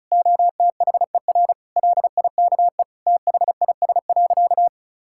Morse code, heard as a keyed tone, about 34 wpm, 700 Hz. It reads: OTHER LIKE THIS.
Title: Sentences from Top 1000 Words - Review Part 6 - 35wpm